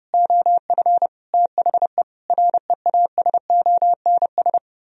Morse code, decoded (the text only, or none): OF THE REASONS